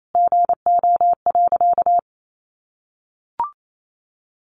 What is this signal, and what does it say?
Morse code, heard as a keyed tone, about 28 wpm, 700 Hz. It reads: GO.
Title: Sentences from Top 600 Words 28wpm